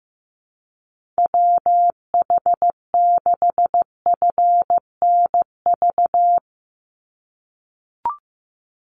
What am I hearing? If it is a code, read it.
Morse code, heard as a keyed tone, about 15 wpm, 700 Hz. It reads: WH6FNV
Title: USA Call Signs 15wpm